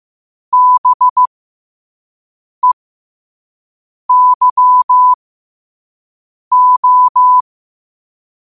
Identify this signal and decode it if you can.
Morse code, heard as a keyed tone, about 15 wpm, spaced as for 5 wpm, 1000 Hz. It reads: BEYO
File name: qotdfiles / 5wpm